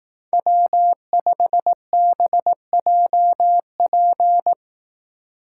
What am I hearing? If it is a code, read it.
Morse code, heard as a keyed tone, about 18 wpm, 700 Hz. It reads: W5BJP
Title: Speed Racing - USA 1x3 Call Signs 15wpm